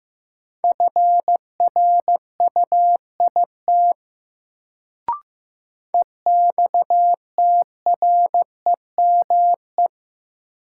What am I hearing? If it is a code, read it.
Morse code, heard as a keyed tone, about 15 wpm, 700 Hz. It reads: FRUIT EXTREME
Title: Sentences from Top 1300 Words - New Word Drill - 15wpm